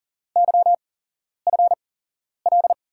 Morse code, heard as a keyed tone, about 40 wpm, spaced as for 11 wpm, 700 Hz. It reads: YFL
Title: USA Call Signs - 5 Characters - 8x Character Spacing - 40wpm